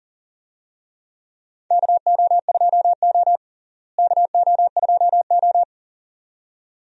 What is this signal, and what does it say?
Morse code, heard as a keyed tone, about 40 wpm, 700 Hz. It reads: KO2O KO2O
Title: USA Call Signs - 4 Characters - Repeated 2x - 40wpm